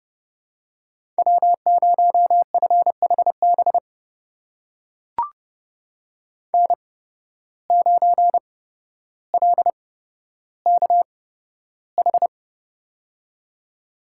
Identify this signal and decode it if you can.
Morse code, heard as a keyed tone, about 30 wpm, spaced as for 13 wpm, 700 Hz. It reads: W0FHB N9LKH
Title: USA Call Signs - 5 Characters - 8x Character Spacing - 30wpm